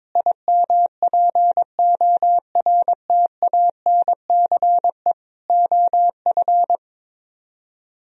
Morse code, heard as a keyed tone, about 23 wpm, 700 Hz. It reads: IMPORTANCE OF
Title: Speed-Racing - Sets of 3 Words 22wpm